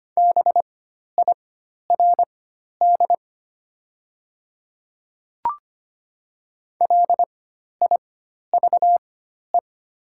Morse code, heard as a keyed tone, about 25 wpm, spaced as for 12 wpm, 700 Hz. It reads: BIRD LIVE
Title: 4x Character Spacing - Top 500 Words - 25wpm